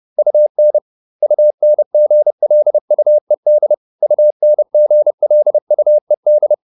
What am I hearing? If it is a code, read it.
Morse code, heard as a keyed tone, about 30 wpm, 600 Hz. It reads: UN UNGLUED UNGLUED